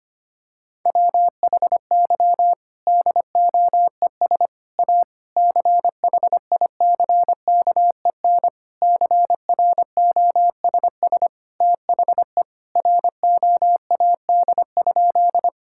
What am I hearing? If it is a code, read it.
Morse code, heard as a keyed tone, about 25 wpm, 700 Hz. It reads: WHY DOES A CHICKEN CROSS THE ROAD?
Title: Sentences from Top 1000 Words - Review Part 5 - 25wpm